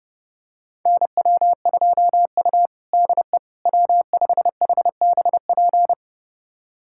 Morse code, heard as a keyed tone, about 30 wpm, 700 Hz. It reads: NW2U DE W5HBP
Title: USA Callsign DE Callsign 30wpm